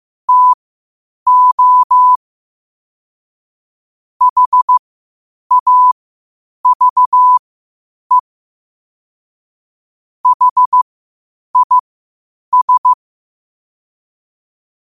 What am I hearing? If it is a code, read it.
Morse code, heard as a keyed tone, about 15 wpm, spaced as for 8 wpm, 1000 Hz. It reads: TO HAVE HIS
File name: qotdfiles / 8wpm